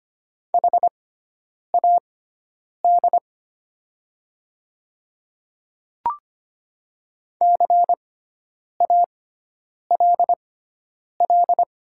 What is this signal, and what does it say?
Morse code, heard as a keyed tone, about 25 wpm, spaced as for 9 wpm, 700 Hz. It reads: HAD CALL
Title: 6x Character Spacing - Top 500 Words - 25wpm